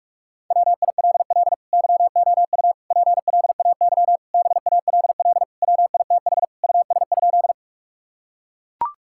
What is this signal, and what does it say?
Morse code, heard as a keyed tone, about 42 wpm, 700 Hz. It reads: WILL YOU PLAY BALL WITH US?
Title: Sentences from Top 1000 Words - Review Part 6 - 45wpm